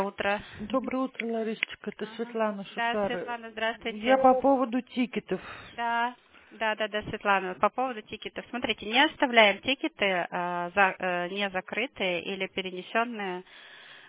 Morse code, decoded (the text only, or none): none